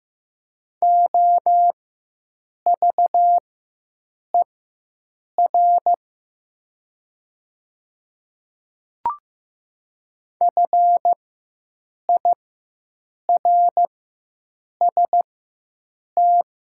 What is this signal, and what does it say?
Morse code, heard as a keyed tone, about 15 wpm, spaced as for 7 wpm, 700 Hz. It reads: OVER FIRST
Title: 4x Character Spacing - Top 100 Words - 15wpm